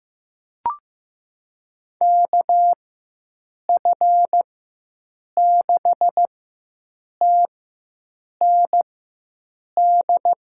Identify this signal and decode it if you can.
Morse code, heard as a keyed tone, about 15 wpm, spaced as for 7 wpm, 700 Hz. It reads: KF6TND
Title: USA Call Signs - 6 Characters - 4x Character Spacing - 15wpm